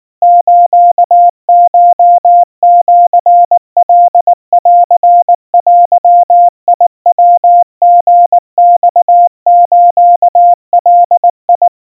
Morse code (Japanese Wabun code), heard as a keyed tone, about 19 wpm, 700 Hz. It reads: スコシカンデヤリマスガ
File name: kawa / Morse sample / kamyu 70 cpm 700 Hz Crispy